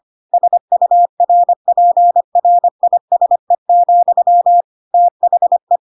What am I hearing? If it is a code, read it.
Morse code, heard as a keyed tone, about 25 wpm, 700 Hz. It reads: SURPRISE, THE
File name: Pinocchio / Pinocchio ch5 25 25 ik1hge